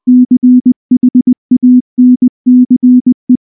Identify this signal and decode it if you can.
Morse code, heard as a keyed tone, about 19 wpm, 250 Hz. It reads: CHANCE
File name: morse / morse-training-20WPM-3